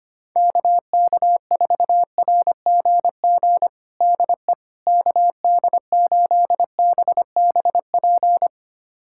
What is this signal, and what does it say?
Morse code, heard as a keyed tone, about 25 wpm, 700 Hz. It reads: KK4RGG DE KD8BBP
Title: USA Callsign DE Callsign 25wpm